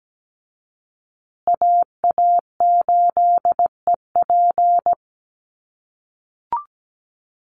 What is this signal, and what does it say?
Morse code, heard as a keyed tone, about 17 wpm, 700 Hz. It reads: AA8EP
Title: USA Call Signs Repeated 3x 17wpm